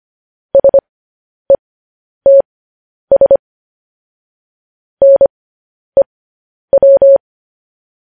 Morse code, heard as a keyed tone, about 25 wpm, spaced as for 10 wpm, 550 Hz. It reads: SETS NEW
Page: news headlines in morse code at 10 wpm